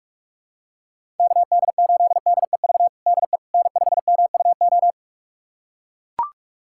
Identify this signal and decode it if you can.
Morse code, heard as a keyed tone, about 44 wpm, 700 Hz. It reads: KD8DEV DE N5MUO
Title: USA Callsign DE Callsign 45wpm